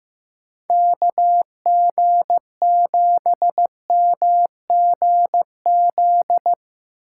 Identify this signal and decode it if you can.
Morse code, heard as a keyed tone, about 15 wpm, 700 Hz. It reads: KG7MGZ
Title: USA Call Signs - 6 Characters - 4x Character Spacing - 15wpm